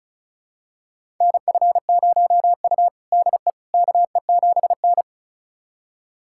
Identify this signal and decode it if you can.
Morse code, heard as a keyed tone, about 36 wpm, 700 Hz. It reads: NF0U DE KE7N
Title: USA Callsign DE Callsign - 4 Characters - 35wpm